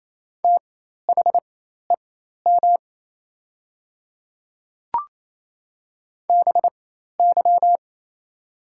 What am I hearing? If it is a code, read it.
Morse code, heard as a keyed tone, about 28 wpm, spaced as for 13 wpm, 700 Hz. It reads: THEM BY
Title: 4x Character Spacing - Top 100 Words - 28wpm